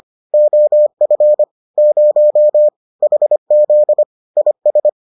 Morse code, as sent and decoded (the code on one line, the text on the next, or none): --- ..-. / ----- / .... --.. / .. ...
OF 0 HZ IS